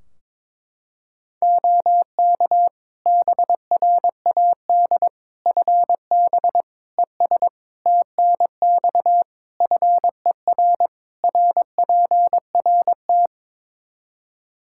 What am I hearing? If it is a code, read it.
Morse code, heard as a keyed tone, about 22 wpm, 700 Hz. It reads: OK BRAD FB ES TNX FER RPRT
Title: K1USN SST QSO Practice 22wpm